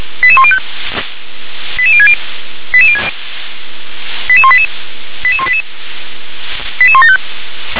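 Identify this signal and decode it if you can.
Five-tone selective call (ZVEI), heard as quick runs of five tones, tones of about 70 ms. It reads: 88105 88060 88188 88188 88175
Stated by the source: ZVEI five-tone call received with an SDR.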